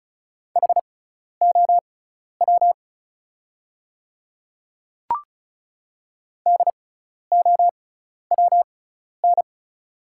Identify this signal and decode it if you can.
Morse code, heard as a keyed tone, about 36 wpm, spaced as for 12 wpm, 690 Hz. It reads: HOW DOWN
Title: 6x Character Spacing - Top 200 Words - 35wpm